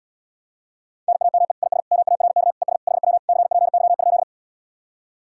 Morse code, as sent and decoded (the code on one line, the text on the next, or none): -. .- -- . / .. ... / -- .- .-. -.- / . ... / .... .-- / -.-. .--. -.-- ..--..
NAME IS MARK ES HW CPY?